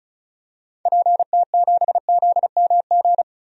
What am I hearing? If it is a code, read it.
Morse code, heard as a keyed tone, about 34 wpm, 700 Hz. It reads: PT7ZMG
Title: DXCC-100 Call Signs - 6 Characters - Repeated 2x - 35wpm